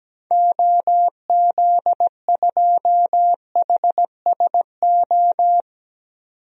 Morse code, heard as a keyed tone, about 17 wpm, 700 Hz. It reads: OZ2HSO
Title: DXCC-100 Call Signs - 6 Characters - 17wpm